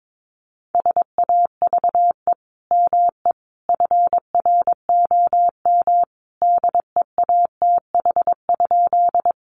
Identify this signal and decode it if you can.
Morse code, heard as a keyed tone, about 22 wpm, 700 Hz. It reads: SAVE ME FROM DEATH?
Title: Sentences from Top 1100 Words 22wpm